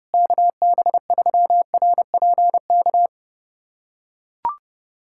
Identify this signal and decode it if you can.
Morse code, heard as a keyed tone, about 32 wpm, 700 Hz. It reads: KB3RPK
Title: USA Callsign DE Callsign 30wpm